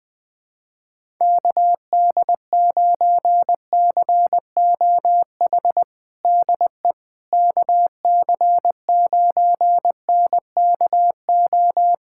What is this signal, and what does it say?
Morse code, heard as a keyed tone, about 21 wpm, 700 Hz. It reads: KD9COH DE KC9NKO
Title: USA Callsign DE Callsign - 6 Characters - 20wpm